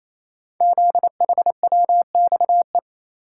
Morse code, heard as a keyed tone, about 27 wpm, 700 Hz. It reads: ZHWXE
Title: Code Groups - 5 Letters - 28wpm